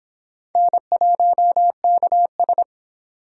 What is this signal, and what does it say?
Morse code, heard as a keyed tone, about 26 wpm, 700 Hz. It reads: N1KS